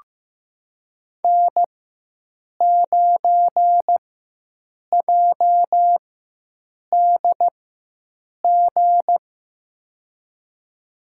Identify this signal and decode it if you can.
Morse code, heard as a keyed tone, about 15 wpm, spaced as for 7 wpm, 700 Hz. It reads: N9JDG
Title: USA Call Signs - 5 Characters - 4x Character Spacing - 15wpm